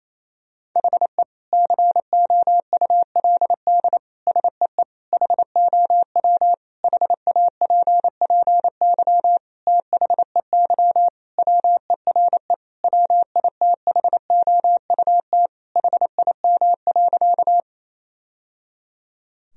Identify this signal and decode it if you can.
Morse code, heard as a keyed tone, about 27 wpm, 700 Hz. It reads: HE COULD SEE HOW HAPPY THEY WERE WITHOUT HIM.
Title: Sentences from Top 1000 Words - Review Part 3 - 28wpm